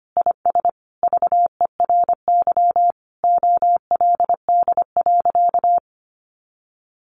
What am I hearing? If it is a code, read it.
Morse code, heard as a keyed tone, about 25 wpm, 700 Hz. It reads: IS VERY OLD.
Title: Sentences from Top 1100 Words 25wpm